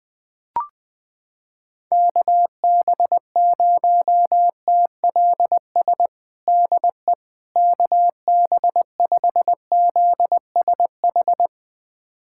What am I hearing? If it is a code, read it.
Morse code, heard as a keyed tone, about 20 wpm, 700 Hz. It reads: KB0TLS DE KB5ZSH
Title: USA Callsign DE Callsign - 6 Characters - 20wpm